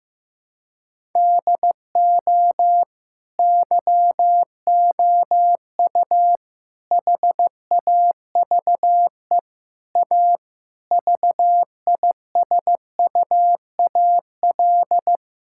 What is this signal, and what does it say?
Morse code, heard as a keyed tone, about 15 wpm, 700 Hz. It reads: DO YOU HAVE A VISUAL
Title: Sentences from Top 1000 Words - Review Part 6 - 15wpm